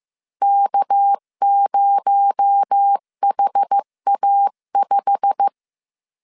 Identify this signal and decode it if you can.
Morse code, heard as a keyed tone, about 15 wpm, 800 Hz. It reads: K0HA5
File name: CW / DET / 1 9 1500C S B 14wpm 3302